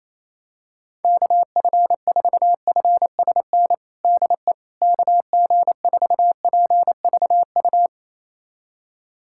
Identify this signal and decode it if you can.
Morse code, heard as a keyed tone, about 27 wpm, 700 Hz. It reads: KF4FSN DE KG4PVU
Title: USA Callsign DE Callsign - 6 Characters - 28wpm